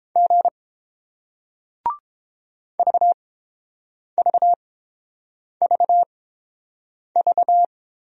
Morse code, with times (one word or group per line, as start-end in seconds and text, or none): 0.2-0.5 G
2.8-7.6 VVVV